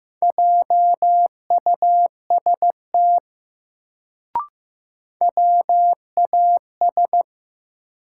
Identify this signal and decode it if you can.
Morse code, heard as a keyed tone, about 15 wpm, 700 Hz. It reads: JUST WAS
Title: Sets of 2 Words - New Word Drill - 15wpm